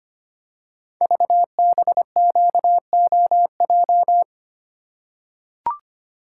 Morse code, heard as a keyed tone, about 25 wpm, 700 Hz. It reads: VBQOJ
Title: Code Groups - 5 Letters - 25wpm